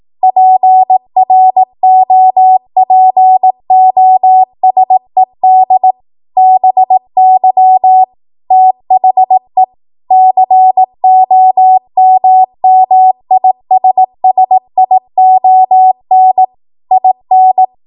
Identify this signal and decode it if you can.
Morse code, heard as a keyed tone, about 18 wpm, 750 Hz. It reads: PROPOSED BY THE COMMISSION IN
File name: Morse / 18WPM